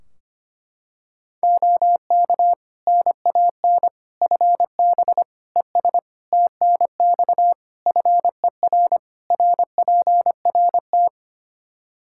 Morse code, heard as a keyed tone, about 25 wpm, 700 Hz. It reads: OK NAN FB ES TNX FER RPRT
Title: Practice QSO - Exchange 2 - Info - 25wpm